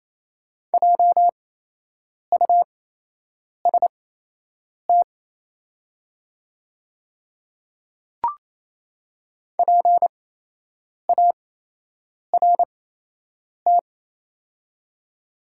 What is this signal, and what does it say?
Morse code, heard as a keyed tone, about 28 wpm, spaced as for 8 wpm, 700 Hz. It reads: JUST PART
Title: 8x Character Spacing - Top 300 Words - 28wpm